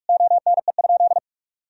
Morse code, heard as a keyed tone, about 41 wpm, 700 Hz. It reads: ONE?
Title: Sentences from Top 1000 Words - Review Part 4 - 45wpm